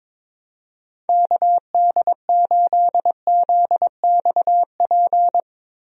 Morse code, heard as a keyed tone, about 22 wpm, 700 Hz. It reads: KD8ZXP
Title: USA Call Signs - 6 Characters - Repeated 3x - 22wpm